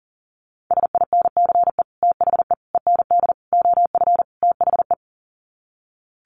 Morse code, heard as a keyed tone, about 40 wpm, 700 Hz. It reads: SINCE THE END OF THE
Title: Sets of 5 Words Encore 40wpm